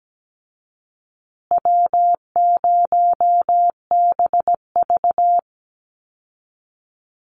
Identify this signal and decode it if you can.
Morse code, heard as a keyed tone, about 17 wpm, 700 Hz. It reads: W0BV
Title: Easy Call Signs Repeated 3x - 17wpm